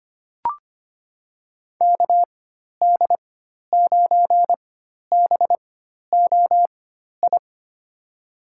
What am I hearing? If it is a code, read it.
Morse code, heard as a keyed tone, about 25 wpm, spaced as for 12 wpm, 700 Hz. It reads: KD9BOI
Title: USA Call Signs - 6 Characters - 4x Character Spacing - 25wpm